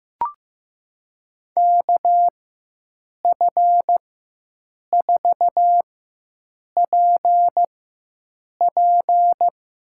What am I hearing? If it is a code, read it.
Morse code, heard as a keyed tone, about 15 wpm, spaced as for 7 wpm, 700 Hz. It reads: KF4PP
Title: USA Call Signs - 4x Character Spacing - 15wpm